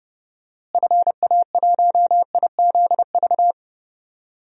Morse code, heard as a keyed tone, about 30 wpm, 700 Hz. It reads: FA1IZV
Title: DXCC-100 Call Signs - 6 Characters - Repeated 3x - 30wpm